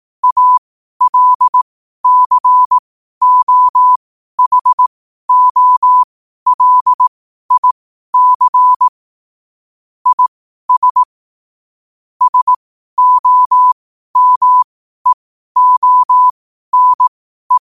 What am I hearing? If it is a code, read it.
Morse code, heard as a keyed tone, about 18 wpm, spaced as for 12 wpm, 1000 Hz. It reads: ALCOHOLIC IS SOMEONE
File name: qotdfiles / 12wpm